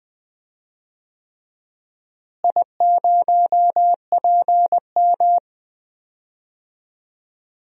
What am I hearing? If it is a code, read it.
Morse code, heard as a keyed tone, about 21 wpm, 700 Hz. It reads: I0PM
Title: DXCC-100 Call Signs - 4 Characters - Repeated 3x - 20wpm